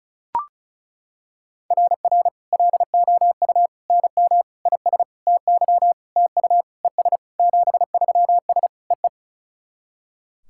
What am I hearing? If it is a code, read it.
Morse code, heard as a keyed tone, about 35 wpm, 700 Hz. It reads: RR LOU NM IS TY TU ES 73S EE
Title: SOTA QSO Practice 35wpm